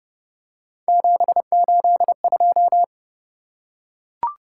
Morse code, heard as a keyed tone, about 29 wpm, 700 Hz. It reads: 782